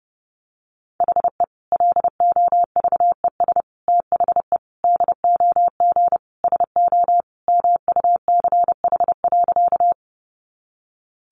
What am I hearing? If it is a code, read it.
Morse code, heard as a keyed tone, about 30 wpm, 700 Hz. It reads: HE LOVES THE DOG SO MUCH.
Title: Additional Easy Phrases 30wpm